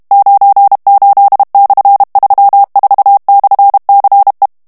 Morse code, heard as a keyed tone, about 32 wpm, 800 Hz. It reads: 98/34/CE
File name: ESPAGNOL TEXTE EN CLAIR 2 VITESSE 30 MOTS MINUTES